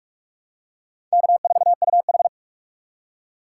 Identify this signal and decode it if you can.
Morse code, heard as a keyed tone, about 43 wpm, 700 Hz. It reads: K4UH
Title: USA Call Signs - 4 Characters - 4x Character Spacing - 45wpm